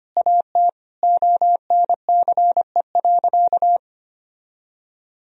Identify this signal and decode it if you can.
Morse code, heard as a keyed tone, about 25 wpm, 700 Hz. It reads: AT ONCE.
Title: Sentences from Top 1000 Words - Review Part 1 - 25wpm